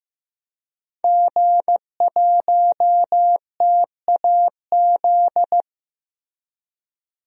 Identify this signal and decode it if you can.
Morse code, heard as a keyed tone, about 15 wpm, 700 Hz. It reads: G1TAZ
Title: DXCC-100 Call Signs - 5 Characters - Repeated 2x - 15wpm